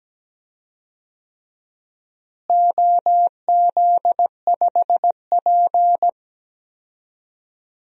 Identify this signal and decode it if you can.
Morse code, heard as a keyed tone, about 17 wpm, 700 Hz. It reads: OZ5P
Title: DXCC-100 Call Signs - 4 Characters - Repeated 3x - 17wpm